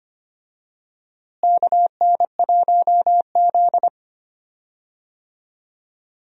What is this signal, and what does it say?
Morse code, heard as a keyed tone, about 25 wpm, 700 Hz. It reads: KN1Z
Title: USA Call Signs - 4 Characters - Repeated 3x - 25wpm